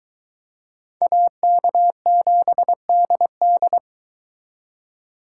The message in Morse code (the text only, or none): AK7DD